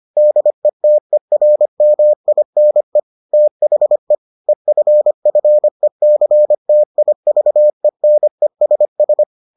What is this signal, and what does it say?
Morse code, heard as a keyed tone, about 25 wpm, 600 Hz. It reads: DETERMINE THE EFFECTIVENESS